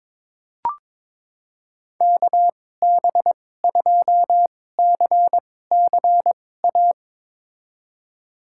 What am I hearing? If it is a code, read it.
Morse code, heard as a keyed tone, about 22 wpm, spaced as for 16 wpm, 700 Hz. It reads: KB2CCA
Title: USA Call Signs - 6 Characters - 2x Character Spacing - 22wpm